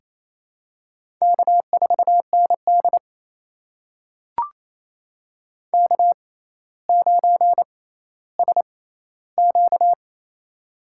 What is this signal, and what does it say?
Morse code, heard as a keyed tone, about 28 wpm, spaced as for 14 wpm, 700 Hz. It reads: K4ND K9SQ
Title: USA Call Signs - 4 Characters - 6x Character Spacing - 28wpm